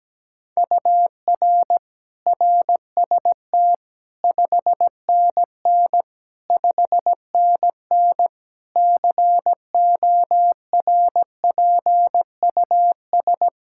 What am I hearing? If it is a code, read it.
Morse code, heard as a keyed tone, about 17 wpm, 700 Hz. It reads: UR RST 5NN 5NN CORPUS
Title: K1USN SST QSO Practice 17wpm